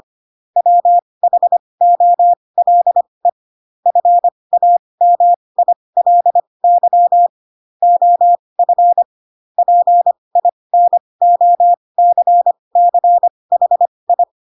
Morse code, heard as a keyed tone, about 25 wpm, spaced as for 20 wpm, 700 Hz. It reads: WHOLE FAMILY OF PINOCCHI